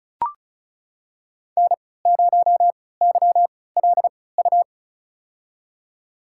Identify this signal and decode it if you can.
Morse code, heard as a keyed tone, about 36 wpm, spaced as for 20 wpm, 700 Hz. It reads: N0YLU
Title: USA Call Signs - 5 Characters - 3x Character Spacing - 35wpm